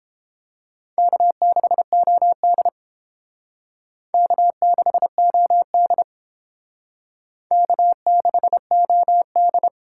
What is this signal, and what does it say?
Morse code, heard as a keyed tone, about 29 wpm, 700 Hz. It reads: K6OD K6OD K6OD